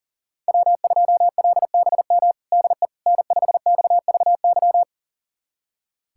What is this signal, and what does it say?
Morse code, heard as a keyed tone, about 40 wpm, 700 Hz. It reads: W2LBM DE N5XVY
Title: USA Callsign DE Callsign 40wpm